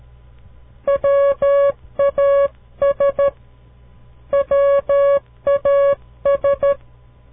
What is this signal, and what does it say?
Morse code, heard as a keyed tone, about 13 wpm, 560 Hz. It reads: WAS WAS
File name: CW Learning Programs / farnsworth / unit 2